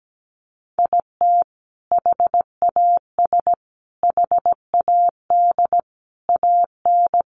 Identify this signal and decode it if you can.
Morse code, heard as a keyed tone, about 17 wpm, 700 Hz. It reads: IT HAS HAD AN
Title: Sentences from Top 1100 Words 17wpm